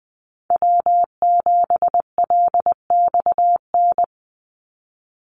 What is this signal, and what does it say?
Morse code, heard as a keyed tone, about 20 wpm, 700 Hz. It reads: W7LXN
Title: Call Signs Repeated 3x - 20wpm